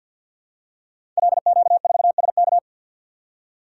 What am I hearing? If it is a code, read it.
Morse code, heard as a keyed tone, about 50 wpm, 700 Hz. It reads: LQ4SK
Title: Code Groups - 5 Letter:Numbers - 50wpm